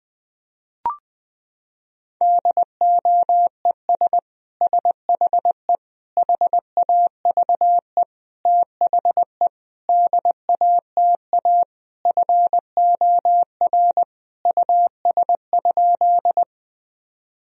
Morse code, heard as a keyed tone, about 21 wpm, 700 Hz. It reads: DOES SHE HAVE THE DATA FOR US?
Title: Sentences from Top 1000 Words - Review Part 3 - 20wpm